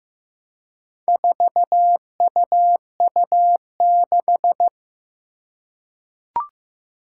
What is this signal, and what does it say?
Morse code, heard as a keyed tone, about 15 wpm, 700 Hz. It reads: 4UU6
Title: Four Challenging Letter-Number Combinations 15wpm